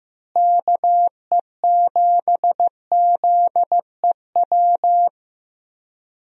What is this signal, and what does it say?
Morse code, heard as a keyed tone, about 15 wpm, 700 Hz. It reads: KE7ZEW
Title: USA Call Signs - 6 Characters - Repeated 3x - 15wpm